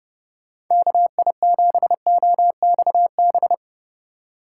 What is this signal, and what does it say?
Morse code, heard as a keyed tone, about 29 wpm, 700 Hz. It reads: KI7OXB
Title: USA Call Signs - 4x Character Spacing - 30wpm